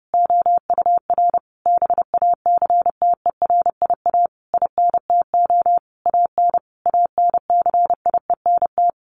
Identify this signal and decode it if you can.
Morse code, heard as a keyed tone, about 30 wpm, 700 Hz. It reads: OUR BACTERIA INTO AN ANCIENT
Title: A Sound Of Thunder 30wpm